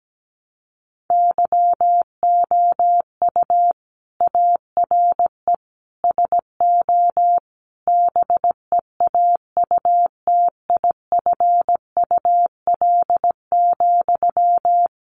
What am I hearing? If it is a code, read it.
Morse code, heard as a keyed tone, about 17 wpm, 700 Hz. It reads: YOU ARE SO BEAUTIFUL,
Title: Sentences from Top 1200 Words 17wpm